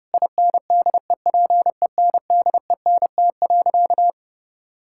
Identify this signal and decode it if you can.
Morse code, heard as a keyed tone, about 30 wpm, 700 Hz. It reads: INDEPENDENT.
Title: Sentences from Top 1000 Words - Review Part 6 - 30wpm